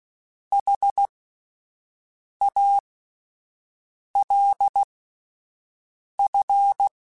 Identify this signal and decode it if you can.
Morse code, heard as a keyed tone, about 16 wpm, spaced as for 5 wpm, 800 Hz. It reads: HALF
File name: Alphabetical / Alphabetical mp3s / Hs @5